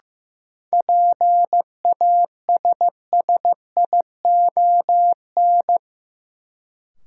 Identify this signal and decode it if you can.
Morse code, heard as a keyed tone, about 16 wpm, 700 Hz. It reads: PASSION
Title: Sentences from Top 1300 Words - New Word Drill Alt - 15wpm